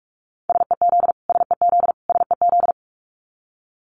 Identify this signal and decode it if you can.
Morse code, heard as a keyed tone, about 46 wpm, 700 Hz. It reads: SEZ SEZ SEZ
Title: Abbreviations Repeated 3x - 45wpm